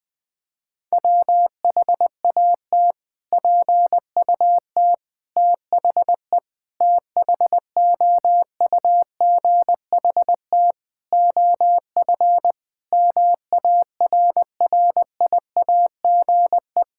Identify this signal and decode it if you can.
Morse code, heard as a keyed tone, about 20 wpm, 700 Hz. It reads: WHAT PUT THE THOUGHT OF MARRIAGE
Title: Sentences from Top 1000 Words - Review Part 5 - 20wpm